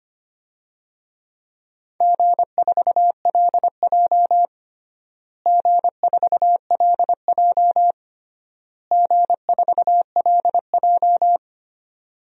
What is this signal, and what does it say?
Morse code, heard as a keyed tone, about 25 wpm, 700 Hz. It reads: G4LJ G4LJ G4LJ
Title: DXCC-100 Call Signs - 4 Characters - Repeated 3x - 25wpm